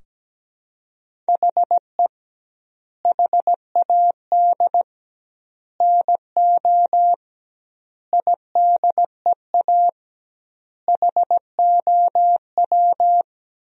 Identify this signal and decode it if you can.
Morse code, heard as a keyed tone, about 17 wpm, 700 Hz. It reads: HE HAD NO IDEA HOW
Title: 2x Word-Spacing - Sentences from Top 300 Words - 17wpm